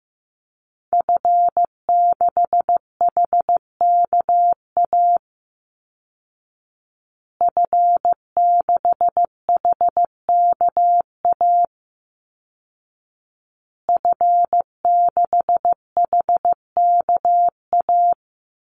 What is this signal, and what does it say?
Morse code, heard as a keyed tone, about 14 wpm, 700 Hz. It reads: F6HKA F6HKA F6HKA
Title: Call Signs Repeated 2x - 15wpm